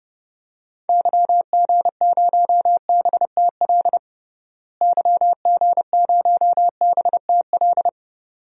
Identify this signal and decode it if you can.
Morse code, heard as a keyed tone, about 30 wpm, 700 Hz. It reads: YG0BTL YG0BTL
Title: DXCC-100 Call Signs - 6 Characters - Repeated 2x - 30wpm